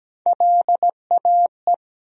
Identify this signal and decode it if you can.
Morse code, heard as a keyed tone, about 17 wpm, 700 Hz. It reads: LAE
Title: USA Call Sign Postfixes - 3 Letters 17wpm